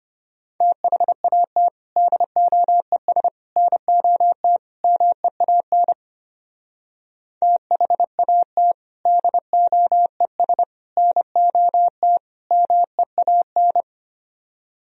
Morse code, heard as a keyed tone, about 31 wpm, 700 Hz. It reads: THAT DOES NOT MEAN THAT DOES NOT MEAN
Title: Speed-Racing - Sets of 4 Words 25wpm